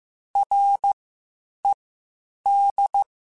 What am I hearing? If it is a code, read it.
Morse code, heard as a keyed tone, about 15 wpm, spaced as for 8 wpm, 800 Hz. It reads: RED
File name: Misc / Misc-mp3s / WordPairs sp @13